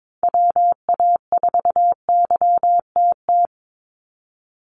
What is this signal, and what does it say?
Morse code, heard as a keyed tone, about 22 wpm, 700 Hz. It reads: WA4YTT
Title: USA Call Signs 22wpm